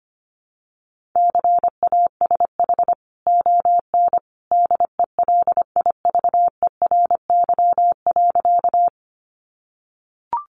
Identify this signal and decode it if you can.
Morse code, heard as a keyed tone, about 25 wpm, 700 Hz. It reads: CASH ON DELIVERY.
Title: Sentences from Top 1500 Words 25wpm